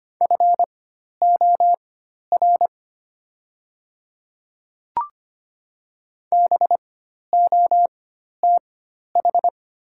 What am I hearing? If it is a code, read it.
Morse code, heard as a keyed tone, about 25 wpm, spaced as for 12 wpm, 700 Hz. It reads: FOR BOTH
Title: 4x Character Spacing - Top 300 Words - 25wpm